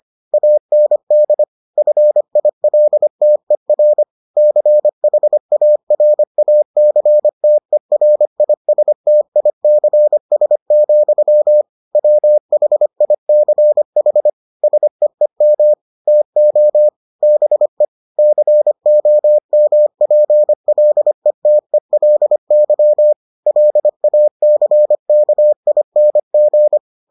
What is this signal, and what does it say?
Morse code, heard as a keyed tone, about 25 wpm, 600 Hz. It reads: AND FILTER CHARACTERISTICS, WHICH SEEM TO BE COMPLETELY LACKING